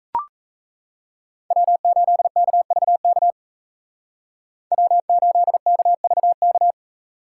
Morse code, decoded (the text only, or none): W8KVK W8KVK